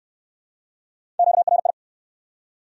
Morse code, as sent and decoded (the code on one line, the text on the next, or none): --.- .-. ..
QRI